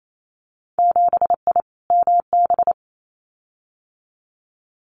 Morse code, heard as a keyed tone, about 28 wpm, 700 Hz. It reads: 7I MB
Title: Winter Field Day Exchanges Encore 28wpm